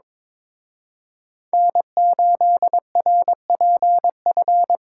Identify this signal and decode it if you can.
Morse code, heard as a keyed tone, about 26 wpm, 700 Hz. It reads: N8RPF
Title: Speed Racing - USA 1x3 Call Signs 22wpm